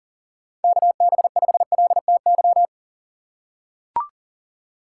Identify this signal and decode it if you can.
Morse code, heard as a keyed tone, about 40 wpm, 700 Hz. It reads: KB5LTY
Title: USA Call Signs - 6 Characters - 6x Character Spacing - 40wpm